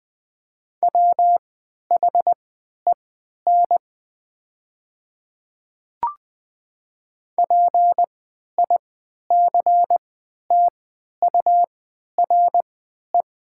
Morse code, heard as a keyed tone, about 20 wpm, spaced as for 11 wpm, 700 Hz. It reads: WHEN PICTURE
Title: 3x Character Spacing - Top 200 Words - 20wpm